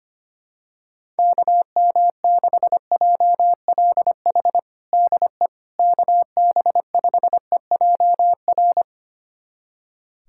Morse code, heard as a keyed tone, about 25 wpm, 700 Hz. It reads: KM6JLH DE KB5EJR
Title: USA Callsign DE Callsign - 6 Characters - 25wpm